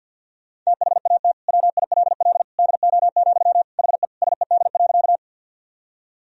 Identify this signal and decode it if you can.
Morse code, heard as a keyed tone, about 49 wpm, 700 Hz. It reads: THAT WILL DO, HE SED.
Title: Sentences from Top 100 Words and Abbreviations 50wpm